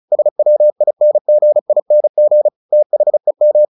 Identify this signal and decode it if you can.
Morse code, heard as a keyed tone, about 35 wpm, 600 Hz. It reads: SWINGING THEM
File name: the old man and the sea / the old man and the sea 1 35wpm